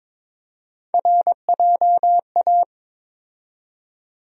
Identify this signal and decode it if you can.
Morse code, heard as a keyed tone, about 22 wpm, 700 Hz. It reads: RJA
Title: USA Call Sign Postfixes - 3 Letters 22wpm